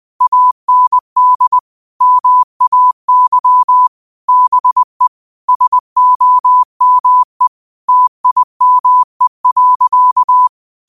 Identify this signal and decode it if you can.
Morse code, heard as a keyed tone, about 20 wpm, 1000 Hz. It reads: AND MAY BE SOME TIME.